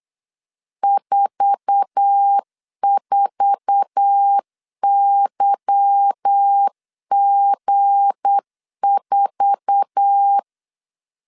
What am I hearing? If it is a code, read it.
Morse code, heard as a keyed tone, about 8 wpm, 800 Hz. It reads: 44YG4
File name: LEZIONI / 8 YJ45 / 8 YJ4 8wpm 3246